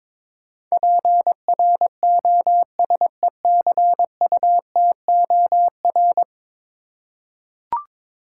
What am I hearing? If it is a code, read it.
Morse code, heard as a keyed tone, about 22 wpm, 700 Hz. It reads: PROSECUTOR
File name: practice / 22wpm / 10 Letter Words - First 3 Letters Given - 22wpm